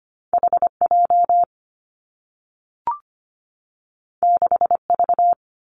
Morse code, heard as a keyed tone, about 25 wpm, 700 Hz. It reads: HJ 6V
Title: Two Challenging Letter-Number Combinations 25wpm